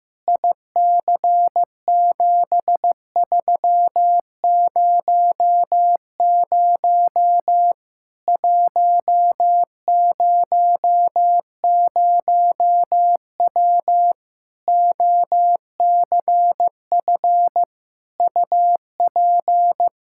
Morse code, heard as a keyed tone, about 15 wpm, 700 Hz. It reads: IC7300 100W OCF UP
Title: Practice QSO - Exchange 2 - Info - 15wpm